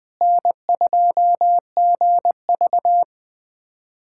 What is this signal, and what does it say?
Morse code, heard as a keyed tone, about 20 wpm, 700 Hz. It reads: N2GV
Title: USA Callsign DE Callsign - 4 Characters - 20wpm